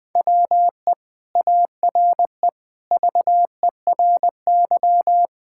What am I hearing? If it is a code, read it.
Morse code, heard as a keyed tone, about 20 wpm, 700 Hz. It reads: WE ARE VERY
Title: Sentences from Top 1000 Words - Review Part 6 - 20wpm